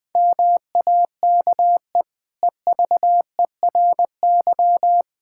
Morse code, heard as a keyed tone, about 20 wpm, 700 Hz. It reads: MAKE EVERY
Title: Sentences from Top 1000 Words - Review Part 5 - 20wpm